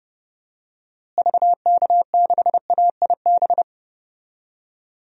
Morse code, heard as a keyed tone, about 30 wpm, 700 Hz. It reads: VK6AIB
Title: DXCC-100 Call Signs - 6 Characters - Repeated 2x - 30wpm